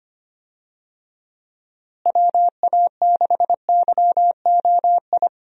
Morse code, heard as a keyed tone, about 25 wpm, 700 Hz. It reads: WA6YOI
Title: USA Call Signs - 6 Characters - Repeated 3x - 25wpm